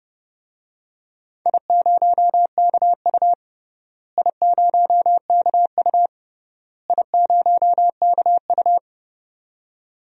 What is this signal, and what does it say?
Morse code, heard as a keyed tone, about 31 wpm, 700 Hz. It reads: I0KU I0KU I0KU
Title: DXCC-100 Call Signs - 4 Characters - Repeated 3x - 30wpm